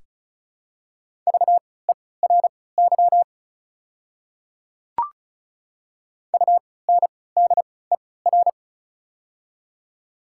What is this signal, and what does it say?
Morse code, heard as a keyed tone, about 34 wpm, spaced as for 20 wpm, 690 Hz. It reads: VERY UNDER
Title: 3x Character Spacing - Top 200 Words - 35wpm